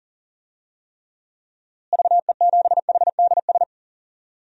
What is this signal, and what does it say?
Morse code, heard as a keyed tone, about 40 wpm, 700 Hz. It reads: VE7HDS
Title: DXCC-100 Call Signs - 6 Characters - Repeated 2x - 40wpm